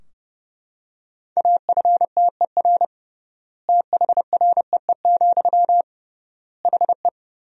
Morse code, heard as a keyed tone, about 30 wpm, 700 Hz. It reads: AFTER THREE, HE
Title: 3x Word-Spacing - Sentences from Top 200 Words - 30wpm